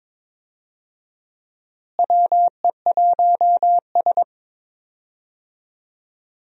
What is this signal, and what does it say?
Morse code, heard as a keyed tone, about 22 wpm, 700 Hz. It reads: WE1S
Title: USA Call Signs - 4 Characters - Repeated 2x - 22wpm